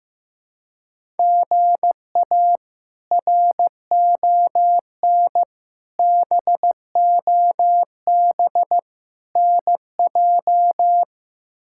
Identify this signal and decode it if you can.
Morse code, heard as a keyed tone, about 15 wpm, 700 Hz. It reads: GA RON BOB NJ
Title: K1USN SST QSO Practice - 15wpm